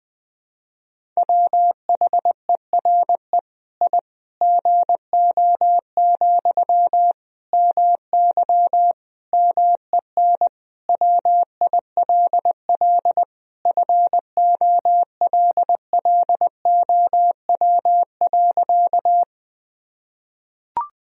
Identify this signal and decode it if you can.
Morse code, heard as a keyed tone, about 20 wpm, 700 Hz. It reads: WHERE I GO, MY MEN WILL FOLLOW.
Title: Sentences from Top 1000 Words - Review Part 1 - 20wpm